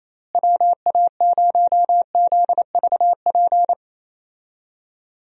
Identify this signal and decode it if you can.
Morse code, heard as a keyed tone, about 28 wpm, 700 Hz. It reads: WA0ZVP
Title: USA Call Signs - 6 Characters - Repeated 2x - 28wpm